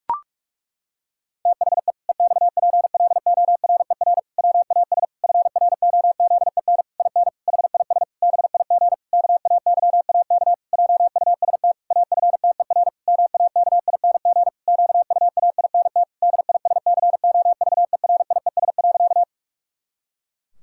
Morse code, heard as a keyed tone, about 46 wpm, 700 Hz. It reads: THE EXPLORER WAS FROZEN IN HIS BIG KAYAK JUST AFTER MAKING QUAINT DISCOVERIES.
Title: Pangram Challenge v2 - 45wpm